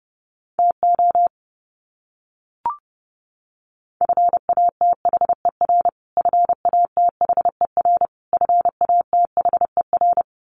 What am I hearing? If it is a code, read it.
Morse code, heard as a keyed tone, about 29 wpm, 700 Hz. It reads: TO FATHER FATHER FATHER
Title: Top 300 Words Repeated 30wpm